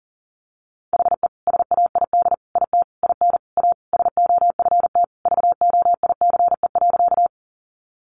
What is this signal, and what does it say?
Morse code, heard as a keyed tone, about 40 wpm, 700 Hz. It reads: HE SAID IT IN A SOFT VOICE.
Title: Sentences from Top 1500 Words 40wpm